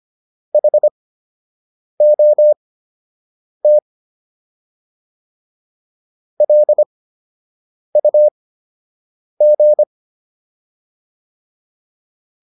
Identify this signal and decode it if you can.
Morse code, heard as a keyed tone, about 25 wpm, spaced as for 7 wpm, 600 Hz. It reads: HOT LUG